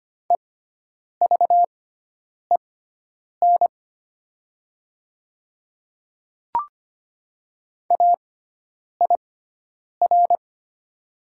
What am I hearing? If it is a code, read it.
Morse code, heard as a keyed tone, about 25 wpm, spaced as for 9 wpm, 700 Hz. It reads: EVEN AIR
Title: 6x Character Spacing - Top 200 Words - 25wpm